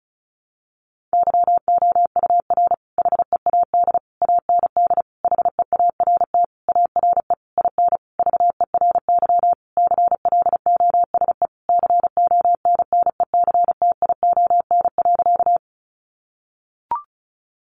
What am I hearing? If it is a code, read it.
Morse code, heard as a keyed tone, about 34 wpm, 700 Hz. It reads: YOUR HEAD AND HEART ARE IN VERY CLOSE CONNECTION.
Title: Sentences from Top 1300 Words 35wpm